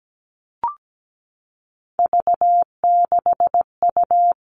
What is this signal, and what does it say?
Morse code, heard as a keyed tone, about 17 wpm, 700 Hz. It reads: V6U
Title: Three Challenging Letter-Number Combinations 17wpm